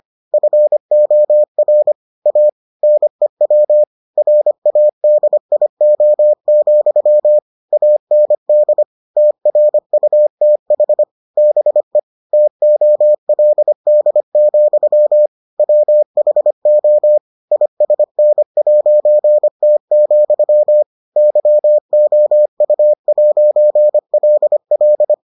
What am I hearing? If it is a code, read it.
Morse code, heard as a keyed tone, about 25 wpm, 600 Hz. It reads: FOR A NEW RADIO, AND TRUTH BE TOLD, WHO ISN'T, YOU'LL